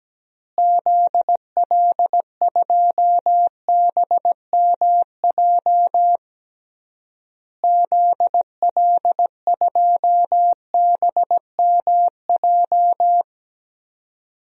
Morse code, heard as a keyed tone, about 17 wpm, 700 Hz. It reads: ZL2BMJ ZL2BMJ
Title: DXCC-100 Call Signs - 6 Characters - Repeated 3x - 17wpm